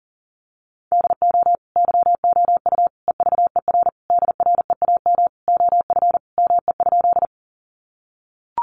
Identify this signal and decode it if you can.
Morse code, heard as a keyed tone, about 40 wpm, 700 Hz. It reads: DO YOU EVER DREAM OF ME?
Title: Sentences from Top 1500 Words 40wpm